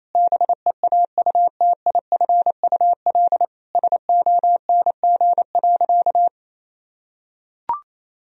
Morse code, heard as a keyed tone, about 28 wpm, 700 Hz. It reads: BEAUTIFUL SONG.
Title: Sentences from Top 1000 Words - Review Part 2 - 28wpm